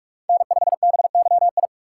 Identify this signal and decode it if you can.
Morse code, heard as a keyed tone, about 46 wpm, 700 Hz. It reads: N5BYI
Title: USA Call Signs - 4x Character Spacing - 45wpm